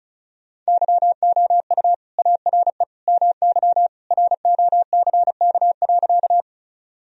Morse code, heard as a keyed tone, about 36 wpm, 700 Hz. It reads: YOU ARE MY ROCK.
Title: Sentences from Top 1000 Words - Review Part 2 - 35wpm